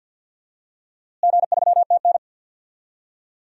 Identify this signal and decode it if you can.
Morse code, heard as a keyed tone, about 52 wpm, 700 Hz. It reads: G3TN
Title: DXCC-100 Call Signs - 4 Characters - Repeated 3x - 50wpm